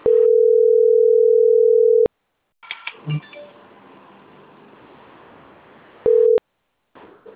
Ringback tone (North America), 440 + 480 Hz.